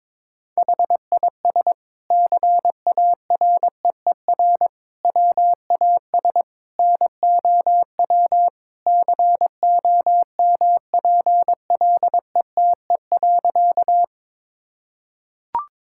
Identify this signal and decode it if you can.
Morse code, heard as a keyed tone, about 21 wpm, 700 Hz. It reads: HIS CAREER WAS NOW COMPLETE.
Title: Sentences from Top 1000 Words - Review Part 6 - 22wpm